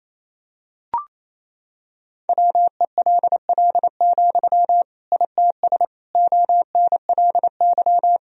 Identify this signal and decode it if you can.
Morse code, heard as a keyed tone, about 28 wpm, 700 Hz. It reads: WELL, ITS ONLY
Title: Sentences from Top 1000 Words - Review Part 3 - 28wpm